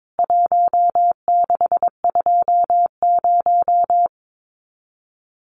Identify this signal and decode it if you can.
Morse code, heard as a keyed tone, about 22 wpm, 700 Hz. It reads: 1620
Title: CWT Exchange 22wpm